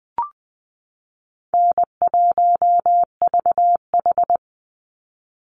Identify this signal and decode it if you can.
Morse code, heard as a keyed tone, about 20 wpm, 700 Hz. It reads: N1VH
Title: Call Signs Repeated 3x - 20wpm